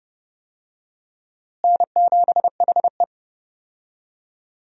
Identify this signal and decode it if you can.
Morse code, heard as a keyed tone, about 30 wpm, 700 Hz. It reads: N7HE
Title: USA Call Signs - 4 Characters - Repeated 2x - 30wpm